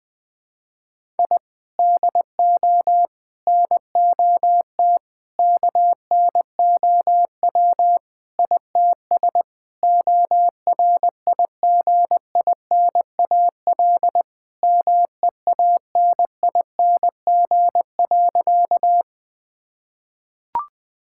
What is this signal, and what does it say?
Morse code, heard as a keyed tone, about 20 wpm, 700 Hz. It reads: I DO NOT KNOW ITS ORIGINAL MEANING.
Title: Sentences from Top 1000 Words - Review Part 6 - 20wpm